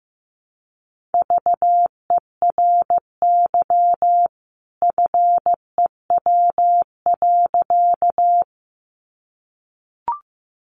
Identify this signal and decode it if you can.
Morse code, heard as a keyed tone, about 15 wpm, 700 Hz. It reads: VERY FEW.
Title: A Sound Of Thunder 15wpm